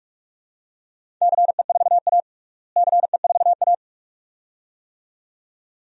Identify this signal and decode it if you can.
Morse code, heard as a keyed tone, about 46 wpm, 700 Hz. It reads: CE4A CE4A
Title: DXCC-100 Call Signs - 4 Characters - Repeated 2x - 45wpm